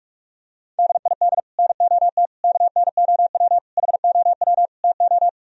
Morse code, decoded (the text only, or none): DID NOT KNOW HOW TO